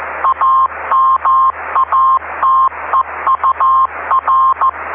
Morse code, heard as a keyed tone, about 15 wpm, 1060 Hz, with noise. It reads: AMATEUR
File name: ve3we-cw-beacon-28265